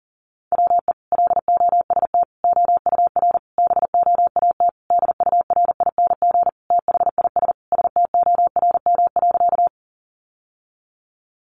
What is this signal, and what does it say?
Morse code, heard as a keyed tone, about 40 wpm, 700 Hz. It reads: WE LOST OUR BOAT DURING THIS STORM.
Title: Sentences from Top 1500 Words 40wpm